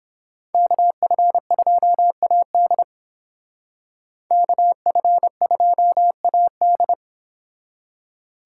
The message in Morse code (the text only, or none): KF2AD KF2AD